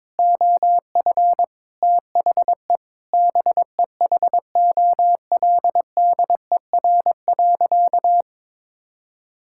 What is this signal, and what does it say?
Morse code, heard as a keyed tone, about 21 wpm, 700 Hz. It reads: OF THE BEHOLDER.
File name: practice / 22wpm / Sentences with 8-Word Phrases - 22wpm